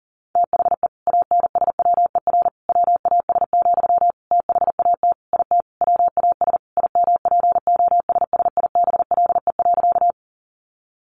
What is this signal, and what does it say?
Morse code, heard as a keyed tone, about 40 wpm, 700 Hz. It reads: THE ANSWER WAS, THAT IT WAS IMPOSSIBLE.
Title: Sentences from Top 1100 Words 40wpm